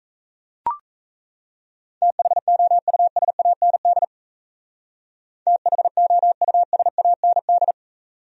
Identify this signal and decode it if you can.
Morse code, heard as a keyed tone, about 37 wpm, 700 Hz. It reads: THOUSAND THOUSAND